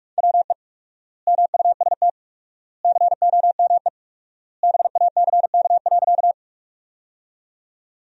Morse code, heard as a keyed tone, about 47 wpm, 700 Hz. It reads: WE MUST COME BACK.
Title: 4x Word-Spacing - Sentences from Top 200 Words - 45wpm